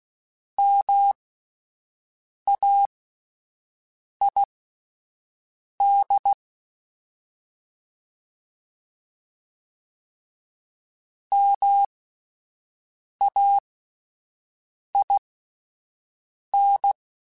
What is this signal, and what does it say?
Morse code, heard as a keyed tone, about 16 wpm, spaced as for 5 wpm, 800 Hz. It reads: MAID MAIN